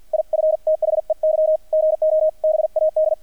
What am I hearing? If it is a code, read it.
Morse code, heard as a keyed tone, about 48 wpm, 630 Hz, with radio noise. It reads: IF THEY GO BAD
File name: archived / CRSnet